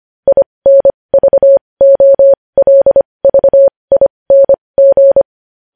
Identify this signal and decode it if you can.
Morse code, heard as a keyed tone, about 25 wpm, spaced as for 20 wpm, 550 Hz. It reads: INVOLVING
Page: news headlines in morse code at 20 wpm